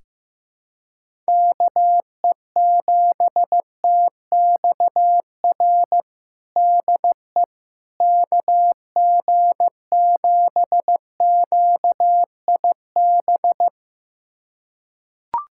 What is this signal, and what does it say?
Morse code, heard as a keyed tone, about 15 wpm, 700 Hz. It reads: KE7TXR DE KG7QIB
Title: USA Callsign DE Callsign - 6 Characters - 15wpm